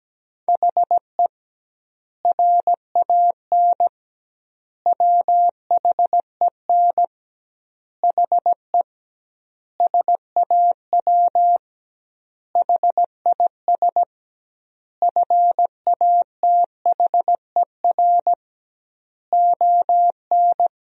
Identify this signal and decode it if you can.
Morse code, heard as a keyed tone, about 17 wpm, 700 Hz. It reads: HE RAN WHEN HE SAW HIS FATHER ON
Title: 2x Word-Spacing - Sentences from Top 500 Words - 17wpm